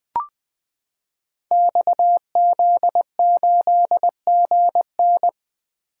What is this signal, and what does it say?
Morse code, heard as a keyed tone, about 20 wpm, 700 Hz. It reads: XZ8GN
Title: Code Groups - 5 Letter:Numbers - 20wpm